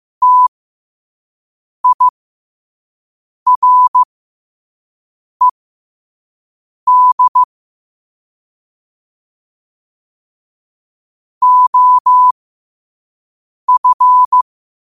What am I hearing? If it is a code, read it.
Morse code, heard as a keyed tone, about 15 wpm, spaced as for 5 wpm, 1000 Hz. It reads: TIRED OF
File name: qotdfiles / 5wpm